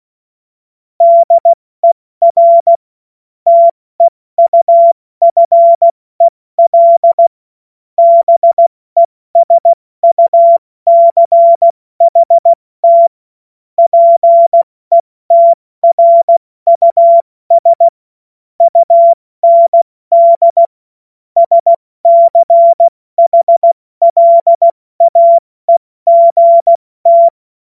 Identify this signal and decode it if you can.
Morse code, heard as a keyed tone, about 16 wpm, 680 Hz. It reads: DER TEUFEL BESUCHT PETRUS UND SCHLAEGT
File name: Morsekurs / Witze / Witz 02. Februar 2021, 70 F80